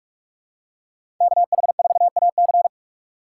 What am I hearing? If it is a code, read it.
Morse code, heard as a keyed tone, about 43 wpm, 700 Hz. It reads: KH4AC
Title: USA Call Signs - 4x Character Spacing - 45wpm